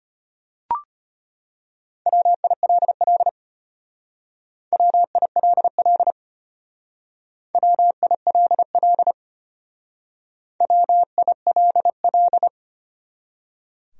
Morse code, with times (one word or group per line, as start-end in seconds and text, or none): 2.1-3.3 WILL
4.7-6.1 WILL
7.5-9.1 WILL
10.6-12.5 WILL